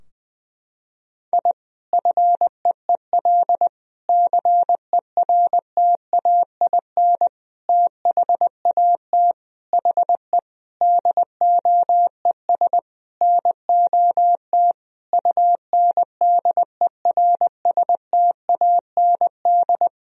Morse code, heard as a keyed tone, about 20 wpm, 700 Hz. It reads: I FEEL CERTAIN THAT HE DOES NOT UNDERSTAND
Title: Sentences from Top 1000 Words - Review Part 2 - 20wpm